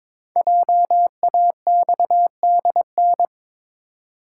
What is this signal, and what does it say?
Morse code, heard as a keyed tone, about 22 wpm, 690 Hz. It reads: JAXDN
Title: Code Groups - 5 Letters - 22wpm